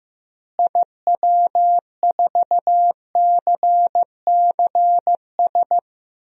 Morse code, heard as a keyed tone, about 15 wpm, 700 Hz. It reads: IW4CCS
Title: DXCC-100 Call Signs - 6 Characters - Repeated 3x - 15wpm